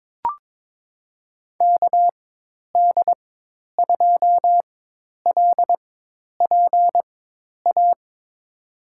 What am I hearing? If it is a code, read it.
Morse code, heard as a keyed tone, about 23 wpm, spaced as for 10 wpm, 700 Hz. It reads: KD2LPA
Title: USA Call Signs - 6 Characters - 4x Character Spacing - 22wpm